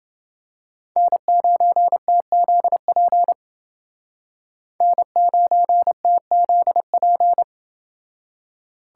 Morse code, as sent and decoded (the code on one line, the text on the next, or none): -. ----. - --.. .--. / -. ----. - --.. .--.
N9TZP N9TZP